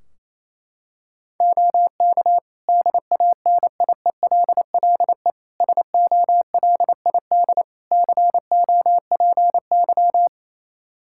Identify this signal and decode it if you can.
Morse code, heard as a keyed tone, about 28 wpm, 700 Hz. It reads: OK DANIELLE SOLID COPY
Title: Practice QSO - Exchange 3 - More Info - 28wpm